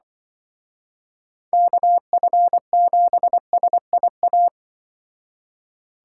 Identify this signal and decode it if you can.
Morse code, heard as a keyed tone, about 24 wpm, 700 Hz. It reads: KF7SIA
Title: Speed Racing - USA 2x3 Call Signs 20wpm